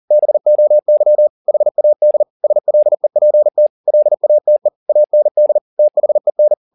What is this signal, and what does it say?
Morse code, heard as a keyed tone, about 40 wpm, 600 Hz. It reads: BOY HAD SLEPT LATE AND THEN
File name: the old man and the sea 2 40wpm